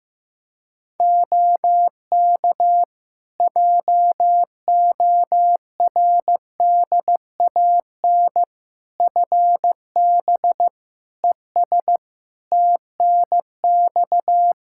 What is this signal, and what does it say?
Morse code, heard as a keyed tone, about 15 wpm, 700 Hz. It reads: OK JORDAN FB ES TNX
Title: Practice QSO - Exchange 2 - Info - 15wpm